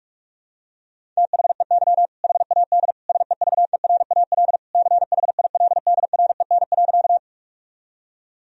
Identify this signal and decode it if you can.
Morse code, heard as a keyed tone, about 41 wpm, 700 Hz. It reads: THEY HAD SEVERAL CHILDREN.
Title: Sentences from Top 1000 Words - Review Part 2 - 45wpm